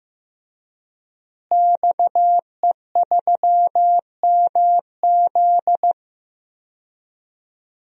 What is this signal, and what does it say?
Morse code, heard as a keyed tone, about 15 wpm, 700 Hz. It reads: XE3MZ
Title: DXCC-100 Call Signs - 5 Characters - Repeated 2x - 15wpm